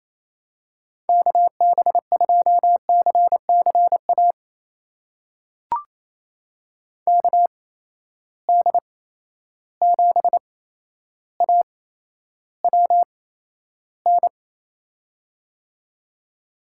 Morse code, heard as a keyed tone, about 28 wpm, spaced as for 12 wpm, 700 Hz. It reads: KB2CCA KD7AWN
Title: USA Call Signs - 8x Character Spacing - 28wpm